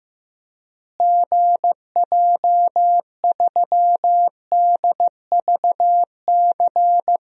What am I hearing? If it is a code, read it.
Morse code, heard as a keyed tone, about 15 wpm, 700 Hz. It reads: GJ3DVC
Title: DXCC-100 Call Signs - 6 Characters - Repeated 2x - 15wpm